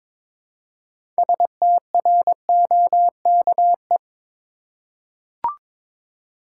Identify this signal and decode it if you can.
Morse code, heard as a keyed tone, about 22 wpm, 700 Hz. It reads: STROKE
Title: Six Letter Words - 22wpm